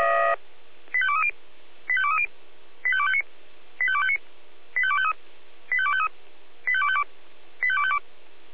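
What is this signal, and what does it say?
Five-tone selective call (ZVEI), heard as quick runs of five tones, tones of about 70 ms. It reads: 85230 85230 85259 85259 85253 85253 85252 85252
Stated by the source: Five-tone ZVEI selective call 85270, received by SDR.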